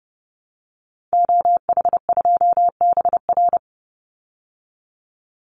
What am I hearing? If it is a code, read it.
Morse code, heard as a keyed tone, about 30 wpm, 700 Hz. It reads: OH2BR
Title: Call Signs Repeated 3x - 30wpm